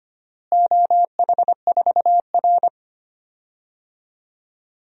Morse code, heard as a keyed tone, about 25 wpm, 700 Hz. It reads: OH4R
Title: DXCC-100 Call Signs - 4 Characters - Repeated 3x - 25wpm